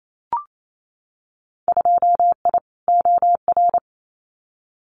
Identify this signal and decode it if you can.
Morse code, heard as a keyed tone, about 28 wpm, 700 Hz. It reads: 2I OR
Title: Winter Field Day Exchanges Encore 28wpm